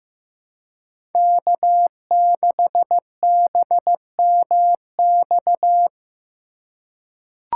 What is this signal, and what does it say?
Morse code, heard as a keyed tone, about 15 wpm, 700 Hz. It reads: K6BMX
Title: USA Call Signs - 5 Characters - 2x Character Spacing - 15wpm